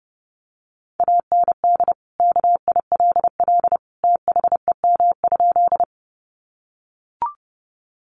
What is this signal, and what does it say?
Morse code, heard as a keyed tone, about 30 wpm, 700 Hz. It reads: AND KILL THEM?
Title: Sentences from Top 1400 Words 30wpm